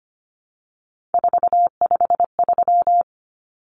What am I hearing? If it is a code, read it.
Morse code, heard as a keyed tone, about 25 wpm, 700 Hz. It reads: 453
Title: Common Cut Numbers - 3-Digit 25wpm